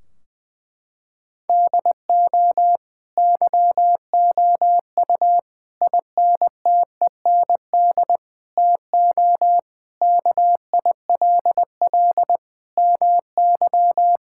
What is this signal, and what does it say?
Morse code, heard as a keyed tone, about 20 wpm, 700 Hz. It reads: DO YOU INTEND TO KILL MY
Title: Sentences from Top 1600 Words 20wpm